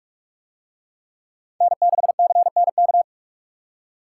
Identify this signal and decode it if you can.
Morse code, heard as a keyed tone, about 44 wpm, 700 Hz. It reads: N6CNK